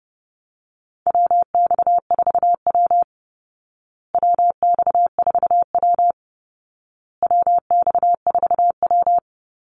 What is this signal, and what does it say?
Morse code, heard as a keyed tone, about 30 wpm, 700 Hz. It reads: WX4W WX4W WX4W
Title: Call Signs Repeated 2x - 30wpm